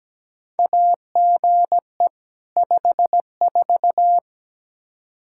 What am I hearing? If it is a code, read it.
Morse code, heard as a keyed tone, about 17 wpm, 700 Hz. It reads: AGE 54
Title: Practice QSO - Exchange 3 - More Info - 17wpm